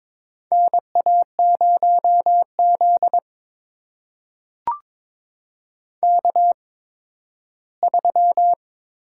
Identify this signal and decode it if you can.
Morse code, heard as a keyed tone, about 22 wpm, spaced as for 13 wpm, 700 Hz. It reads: NA0Z K3
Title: USA Call Signs - 4 Characters - 8x Character Spacing - 22wpm